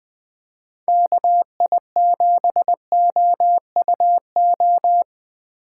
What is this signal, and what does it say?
Morse code, heard as a keyed tone, about 20 wpm, 700 Hz. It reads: KI7OUO
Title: USA Call Signs - 6 Characters - 3x Character Spacing - 20wpm